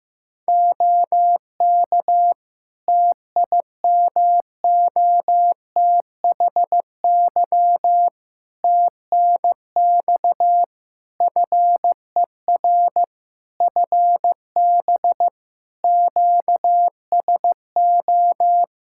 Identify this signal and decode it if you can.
Morse code, heard as a keyed tone, about 15 wpm, 700 Hz. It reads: OK TIMOTHY TNX FER FB QSO
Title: K1USN SST QSO Practice 15wpm